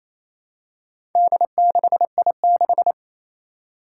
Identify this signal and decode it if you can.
Morse code, heard as a keyed tone, about 28 wpm, 700 Hz. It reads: D6I6
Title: Four Challenging Letter-Number Combinations 28wpm